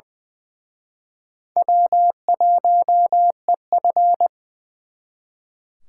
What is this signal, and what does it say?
Morse code, heard as a keyed tone, about 25 wpm, 700 Hz. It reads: W1EF
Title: Speed Racing - USA Call Signs 20wpm